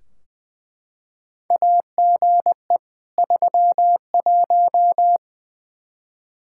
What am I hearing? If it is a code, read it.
Morse code, heard as a keyed tone, about 20 wpm, 700 Hz. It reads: AGE 31
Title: Practice QSO - Exchange 3 - More Info - 20wpm